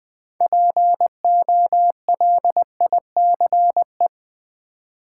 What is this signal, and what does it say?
Morse code, heard as a keyed tone, about 20 wpm, 700 Hz. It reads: POLICE
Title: 6 Letter Words - First 3 Letters Given - 20wpm